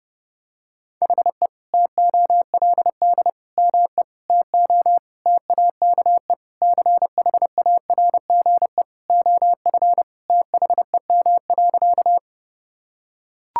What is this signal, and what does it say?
Morse code, heard as a keyed tone, about 30 wpm, 700 Hz. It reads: HE TOLD ME TO TAKE CHARGE OF THEM.
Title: Sentences from Top 1000 Words - Review Part 6 - 30wpm